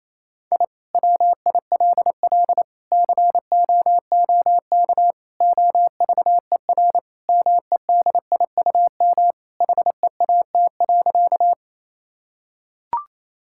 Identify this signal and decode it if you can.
Morse code, heard as a keyed tone, about 28 wpm, 700 Hz. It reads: I WILL COOK OVER MEDIUM HEAT.
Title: Sentences from Top 1000 Words - Review Part 6 - 28wpm